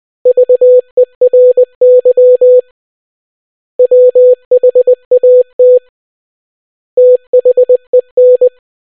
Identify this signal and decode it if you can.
Morse code, heard as a keyed tone, about 20 wpm, 500 Hz. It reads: VERY WHAT THEN